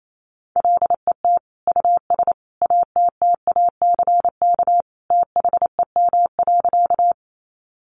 Morse code, heard as a keyed tone, about 28 wpm, 700 Hz. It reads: LET US ATTACK THEM.
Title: Sentences from Top 1500 Words 28wpm